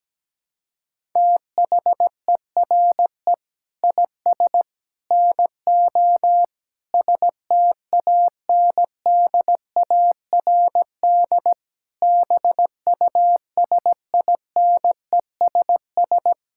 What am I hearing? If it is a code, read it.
Morse code, heard as a keyed tone, about 17 wpm, 700 Hz. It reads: THERE IS NO STANDARD BUSINESS
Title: Sentences from Top 1000 Words - Review Part 6 - 17wpm